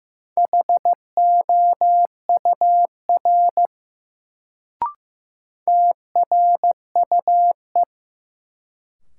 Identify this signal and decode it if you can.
Morse code, heard as a keyed tone, about 15 wpm, 700 Hz. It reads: HOUR TRUE
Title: Sentences from Top 400 Words - New Word Drill - 15wpm